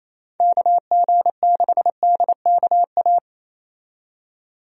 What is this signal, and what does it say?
Morse code, heard as a keyed tone, about 27 wpm, 700 Hz. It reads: KG6DKA
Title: USA Call Signs - 6 Characters - 28wpm